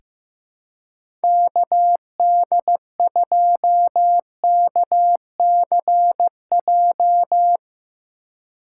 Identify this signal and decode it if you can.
Morse code, heard as a keyed tone, about 15 wpm, 700 Hz. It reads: KD2KCJ
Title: USA Call Signs - 6 Characters - 2x Character Spacing - 15wpm